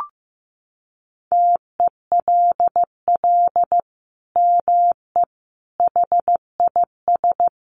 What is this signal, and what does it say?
Morse code, heard as a keyed tone, about 15 wpm, 700 Hz. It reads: TELL ME HIS
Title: Sentences from Top 200 Words 15wpm